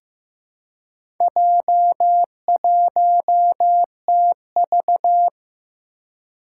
Morse code, heard as a keyed tone, about 15 wpm, 700 Hz. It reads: J1TV
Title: Four Challenging Letter-Number Combinations 15wpm